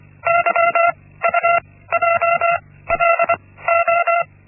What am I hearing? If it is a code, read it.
Morse code, heard as a keyed tone, about 25 wpm, spaced as for 16 wpm, 680 Hz. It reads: YUJLO
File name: Linksmakalnis-1